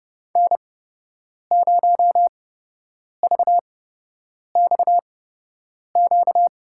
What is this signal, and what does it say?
Morse code, heard as a keyed tone, about 32 wpm, spaced as for 8 wpm, 700 Hz. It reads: N0VXQ
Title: USA Call Signs - 5 Characters - 8x Character Spacing - 30wpm